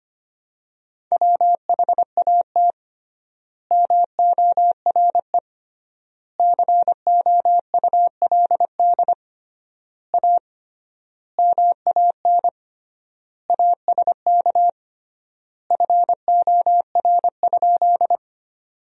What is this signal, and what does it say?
Morse code, heard as a keyed tone, about 25 wpm, 700 Hz. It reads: WHAT MORE COULD A MAN ASK FOR?
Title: 3x Word-Spacing - Sentences from Top 200 Words - 25wpm